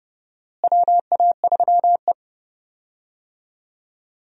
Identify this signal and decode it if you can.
Morse code, heard as a keyed tone, about 31 wpm, 700 Hz. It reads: WA3E
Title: USA Call Signs - 4 Characters - Repeated 2x - 30wpm